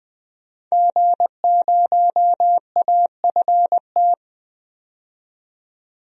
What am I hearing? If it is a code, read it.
Morse code, heard as a keyed tone, about 20 wpm, 700 Hz. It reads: G0AFT
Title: DXCC-100 Call Signs - 5 Characters - 20wpm